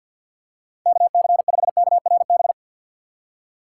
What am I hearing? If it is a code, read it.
Morse code, heard as a keyed tone, about 53 wpm, 700 Hz. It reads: KC5KRB
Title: USA Call Signs - 6 Characters - 4x Character Spacing - 50wpm